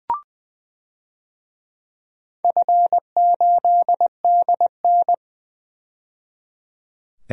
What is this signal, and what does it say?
Morse code, heard as a keyed tone, about 20 wpm, 700 Hz. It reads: F8DN
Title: DXCC-100 Call Signs - 4 Characters - 20wpm